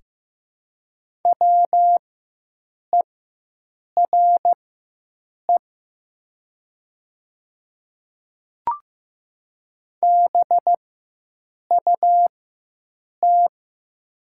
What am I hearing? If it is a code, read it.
Morse code, heard as a keyed tone, about 15 wpm, spaced as for 7 wpm, 700 Hz. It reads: WERE BUT
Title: 4x Character Spacing - Top 100 Words - 15wpm